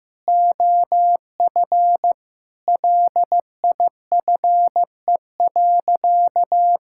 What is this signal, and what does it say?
Morse code, heard as a keyed tone, about 15 wpm, 700 Hz. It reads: OF LIFE.
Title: Sentences from Top 1000 Words - Review Part 5 - 15wpm